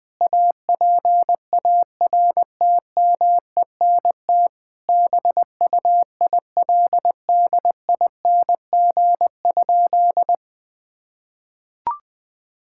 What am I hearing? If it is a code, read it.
Morse code, heard as a keyed tone, about 20 wpm, 700 Hz. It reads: APARTMENT BUILDING?
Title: Sentences from Top 1000 Words - Review Part 6 - 20wpm